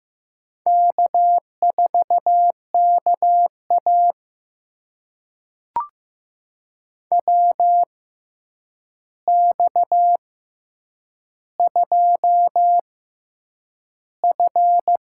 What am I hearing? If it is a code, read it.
Morse code, heard as a keyed tone, about 15 wpm, spaced as for 8 wpm, 700 Hz. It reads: K4KA WX2F
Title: USA Call Signs - 4 Characters - 6x Character Spacing - 15wpm